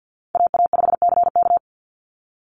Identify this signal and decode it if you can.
Morse code, heard as a keyed tone, about 49 wpm, 700 Hz. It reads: AA5CK
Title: NA SOTA Chaser Callsigns 50wpm